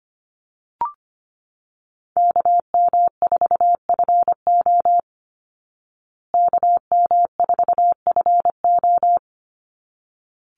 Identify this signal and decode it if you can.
Morse code, heard as a keyed tone, about 25 wpm, 700 Hz. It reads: KM4FO KM4FO
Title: Call Signs Repeated 3x - 25wpm